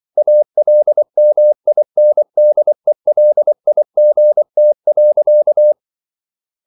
Morse code, heard as a keyed tone, about 24 wpm, 600 Hz. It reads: ALMINDELIGT.
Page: Ferie 24 wpm